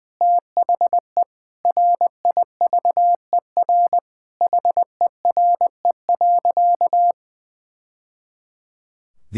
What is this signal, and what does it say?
Morse code, heard as a keyed tone, about 20 wpm, 700 Hz. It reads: THE RIVER HERE.
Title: Sentences from Top 1000 Words - Review Part 5 - 20wpm